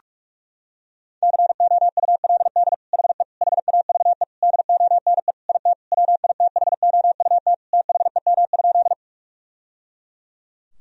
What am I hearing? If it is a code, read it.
Morse code, heard as a keyed tone, about 41 wpm, 700 Hz. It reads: COULD HE HAVE DONE IT WITHOUT THEM?
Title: Sentences from Top 1000 Words - Review Part 3 - 45wpm